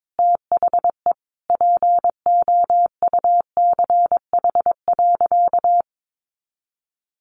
Morse code, heard as a keyed tone, about 22 wpm, 700 Hz. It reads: THE POUCH.
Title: Easy Phrases 22wpm